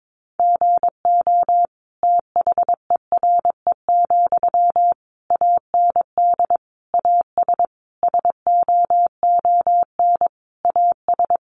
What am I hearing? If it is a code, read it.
Morse code, heard as a keyed tone, about 22 wpm, 700 Hz. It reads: GO THERE, AND AS SOON AS
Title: Sentences from Top 1300 Words 22wpm